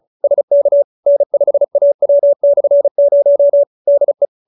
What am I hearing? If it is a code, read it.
Morse code, heard as a keyed tone, about 34 wpm, 570 Hz. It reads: SK N5AW/0 DE
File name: QSO312 35